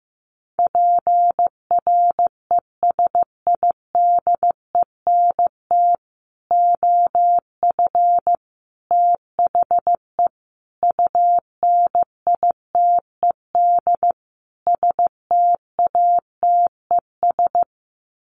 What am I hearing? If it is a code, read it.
Morse code, heard as a keyed tone, about 15 wpm, 700 Hz. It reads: PRESIDENT OF THE UNITED STATES
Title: Sets of 5 Words 15wpm